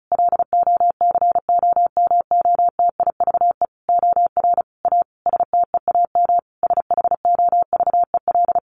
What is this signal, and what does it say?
Morse code, heard as a keyed tone, about 34 wpm, 700 Hz. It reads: LOCOMOTIVE OR A STEAM SHOVEL
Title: A Sound Of Thunder 35wpm